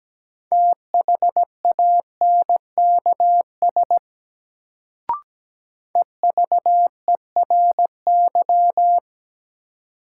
Morse code, heard as a keyed tone, about 17 wpm, 700 Hz. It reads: THANKS EVERY
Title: Sets of 5 Words - New Word Drill - 17wpm